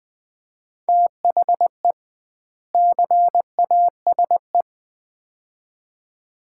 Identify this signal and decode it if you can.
Morse code, heard as a keyed tone, about 20 wpm, 700 Hz. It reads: THE CASE
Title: 2x Word-Spacing - Sets of 2 Words - 20wpm